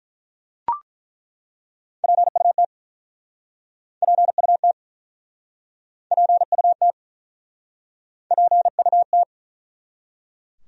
Morse code, heard as a keyed tone, about 49 wpm, 700 Hz. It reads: PUT PUT PUT PUT